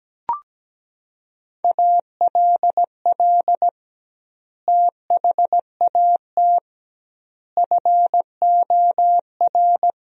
Morse code, heard as a keyed tone, about 17 wpm, 700 Hz. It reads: ALL THAT FOR
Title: 2x Word-Spacing - Sentences from Top 500 Words - 17wpm